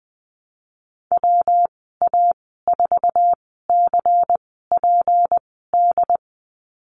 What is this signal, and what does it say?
Morse code, heard as a keyed tone, about 20 wpm, spaced as for 15 wpm, 700 Hz. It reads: WA4CPD
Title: 207 Lesson 33g - Letter j - Call Signs 20wpm